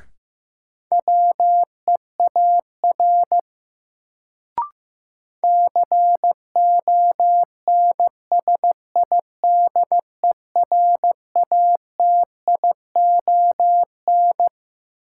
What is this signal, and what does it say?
Morse code, heard as a keyed tone, about 15 wpm, 700 Hz. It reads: WEAR CONSIDERATION
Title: Sentences from Top 1500 Words - New Word Drill Alt - 15wpm